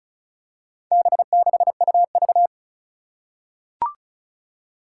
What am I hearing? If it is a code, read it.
Morse code, heard as a keyed tone, about 35 wpm, 700 Hz. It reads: B6UV
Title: Four Challenging Letter-Number Combinations 35wpm